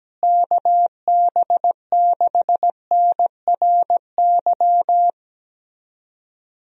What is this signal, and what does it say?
Morse code, heard as a keyed tone, about 17 wpm, 700 Hz. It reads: KB6NRY
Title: USA Call Signs - 6 Characters - Repeated 3x - 17wpm